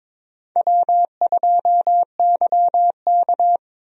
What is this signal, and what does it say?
Morse code, heard as a keyed tone, about 22 wpm, 700 Hz. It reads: W2YK
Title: USA Call Signs - 4 Characters - Repeated 2x - 22wpm